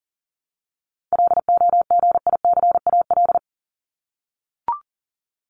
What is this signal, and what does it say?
Morse code, heard as a keyed tone, about 40 wpm, 700 Hz. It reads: LOGICAL
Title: Top 1501-1750 Words 40wpm